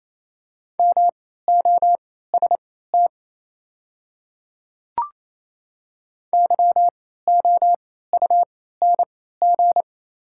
Morse code, heard as a keyed tone, about 28 wpm, spaced as for 16 wpm, 690 Hz. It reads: MOST YOUNG
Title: 3x Character Spacing - Top 500 Words - 28wpm